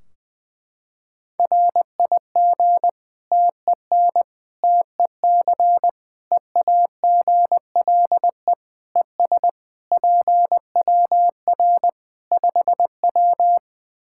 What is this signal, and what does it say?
Morse code, heard as a keyed tone, about 20 wpm, 700 Hz. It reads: RIG TEN TEC EAGLE ES PWR 5W
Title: Practice QSO - Exchange 2 - Info - 20wpm